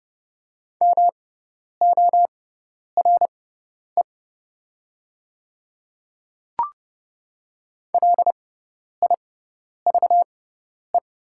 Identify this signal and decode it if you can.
Morse code, heard as a keyed tone, about 30 wpm, spaced as for 10 wpm, 700 Hz. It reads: MORE LIVE
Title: 6x Character Spacing - Top 300 Words - 30wpm